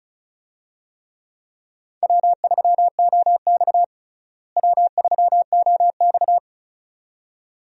Morse code, heard as a keyed tone, about 36 wpm, 700 Hz. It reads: W3OX W3OX
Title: USA Call Signs - 4 Characters - Repeated 2x - 35wpm